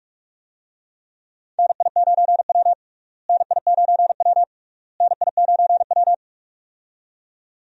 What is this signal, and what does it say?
Morse code, heard as a keyed tone, about 46 wpm, 700 Hz. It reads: NI9W NI9W NI9W
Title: USA Call Signs - 4 Characters - Repeated 3x - 45wpm